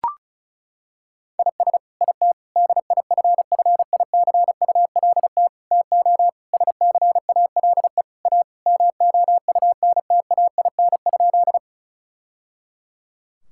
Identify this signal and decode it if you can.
Morse code, heard as a keyed tone, about 34 wpm, 700 Hz. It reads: IS IT DIFFICULT TO SCALE A MOUNTAIN?
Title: Sentences from Top 1000 Words - Review Part 5 - 35wpm